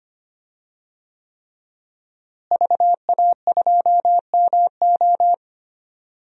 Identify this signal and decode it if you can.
Morse code, heard as a keyed tone, about 25 wpm, 700 Hz. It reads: VA2MO
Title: DXCC-100 Call Signs - 5 Characters - Repeated 3x - 25wpm